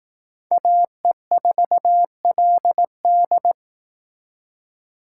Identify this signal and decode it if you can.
Morse code, heard as a keyed tone, about 18 wpm, 700 Hz. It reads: AE4LD
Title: Speed Racing - USA Call Signs 15wpm